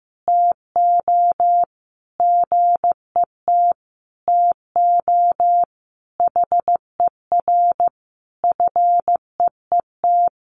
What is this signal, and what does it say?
Morse code, heard as a keyed tone, about 15 wpm, 700 Hz. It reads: TO GET TO HER FEET
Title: Sentences from Top 1300 Words 15wpm